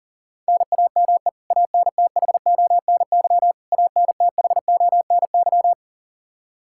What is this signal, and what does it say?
Morse code, heard as a keyed tone, about 40 wpm, 700 Hz. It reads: NAME ANTHONY ANTHONY
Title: Practice QSO - Exchange 1 - Beginning - 40wpm